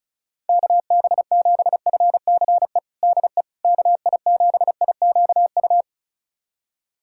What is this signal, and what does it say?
Morse code, heard as a keyed tone, about 36 wpm, 700 Hz. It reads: KB7FCE DE KI7IQU
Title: USA Callsign DE Callsign - 6 Characters - 35wpm